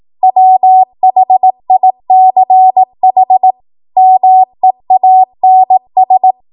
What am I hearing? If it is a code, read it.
Morse code, heard as a keyed tone, about 18 wpm, 750 Hz. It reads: WHICH MEANS
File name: Morse / 18WPM